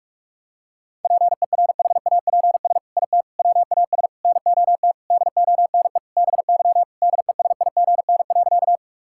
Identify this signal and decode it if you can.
Morse code, heard as a keyed tone, about 46 wpm, 700 Hz. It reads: PERHAPS IT WAS NOT DONE BY DESIGN.
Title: Sentences from Top 1000 Words - Review Part 5 - 45wpm